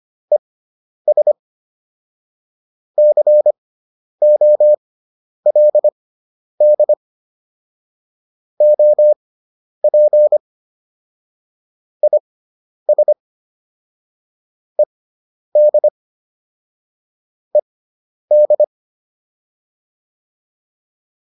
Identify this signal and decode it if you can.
Morse code, heard as a keyed tone, about 25 wpm, spaced as for 10 wpm, 600 Hz. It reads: ES COLD OP IS ED ED
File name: QSO302 10